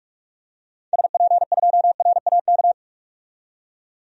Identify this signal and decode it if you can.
Morse code, heard as a keyed tone, about 44 wpm, 700 Hz. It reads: SP2RAK
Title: DXCC-100 Call Signs - 6 Characters - 45wpm